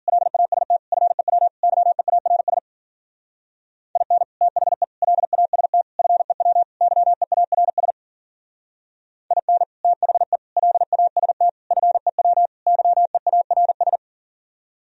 Morse code, heard as a keyed tone, about 46 wpm, 700 Hz. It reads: LAST FEW YEARS IN THE LAST FEW YEARS IN THE LAST FEW YEARS
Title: Speed-Racing - Sets of 5 Words 40wpm